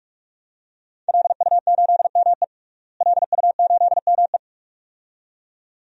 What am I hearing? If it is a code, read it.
Morse code, heard as a keyed tone, about 41 wpm, 700 Hz. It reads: LU8ME LU8ME
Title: DXCC-100 Call Signs - 5 Characters - Repeated 2x - 45wpm